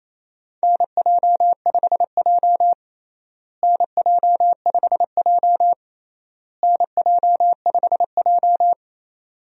Morse code, heard as a keyed tone, about 28 wpm, 700 Hz. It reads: NJ5J NJ5J NJ5J
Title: USA Call Signs - 4 Characters - Repeated 3x - 28wpm